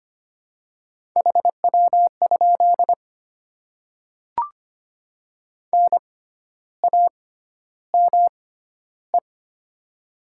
Morse code, heard as a keyed tone, about 25 wpm, spaced as for 12 wpm, 700 Hz. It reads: HW? NAME
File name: Top 20 QSO Elements - 6x Character Spacing - 25wpm